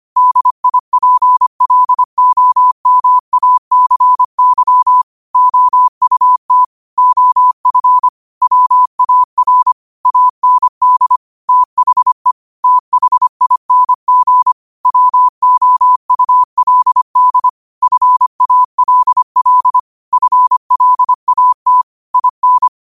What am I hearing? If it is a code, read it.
Morse code, heard as a keyed tone, about 25 wpm, 1000 Hz. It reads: DIPLOMACY OUT OF WAR AND THE THING WOULD FALL FLAT IN
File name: qotdfiles / 25wpm